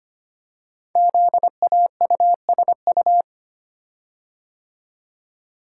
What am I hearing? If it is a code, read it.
Morse code, heard as a keyed tone, about 25 wpm, 700 Hz. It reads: ZAUSU
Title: Code Groups - 5 Letters - 25wpm